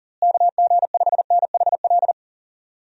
Morse code, heard as a keyed tone, about 38 wpm, 700 Hz. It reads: KG5NHL
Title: USA Call Signs - 6 Characters - Repeated 2x - 40wpm